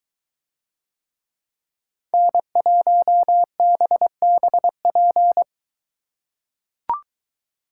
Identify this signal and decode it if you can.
Morse code, heard as a keyed tone, about 21 wpm, 700 Hz. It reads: N1BBP